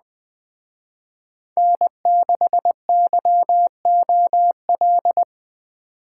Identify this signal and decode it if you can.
Morse code, heard as a keyed tone, about 20 wpm, 700 Hz. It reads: N6YOL
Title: Speed Racing - USA 1x3 Call Signs 17wpm